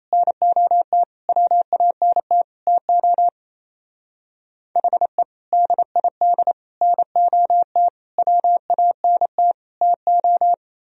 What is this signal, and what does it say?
Morse code, heard as a keyed tone, about 32 wpm, 700 Hz. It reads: NOT WANT TO HE DID NOT WANT TO
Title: Speed-Racing - Sets of 5 Words 28wpm